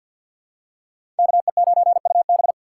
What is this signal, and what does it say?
Morse code, heard as a keyed tone, about 51 wpm, 700 Hz. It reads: KE9UB
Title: USA Call Signs - 5 Characters - 2x Character Spacing - 50wpm